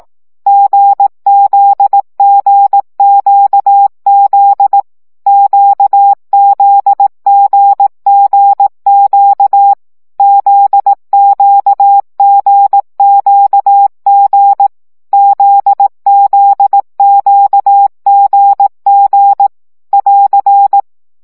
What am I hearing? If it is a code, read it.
Morse code, heard as a keyed tone, about 18 wpm, 800 Hz. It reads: GZGQZ QZGGQ ZQGQG ZZQGG <AR>
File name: CW gyakorlas g z q 15 baud